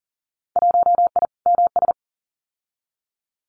Morse code, heard as a keyed tone, about 40 wpm, 700 Hz. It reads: 1I MS
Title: Winter Field Day Exchanges Encore 40wpm